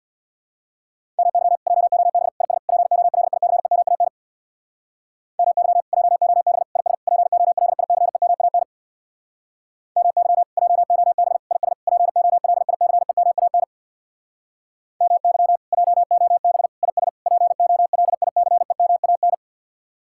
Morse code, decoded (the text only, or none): MY JOB IS POLICEMAN MY JOB IS POLICEMAN MY JOB IS POLICEMAN MY JOB IS POLICEMAN